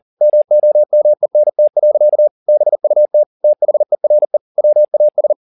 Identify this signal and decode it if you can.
Morse code, heard as a keyed tone, about 40 wpm, 600 Hz. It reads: MOMENT. BUT THERE WAS
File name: the old man and the sea 2 40wpm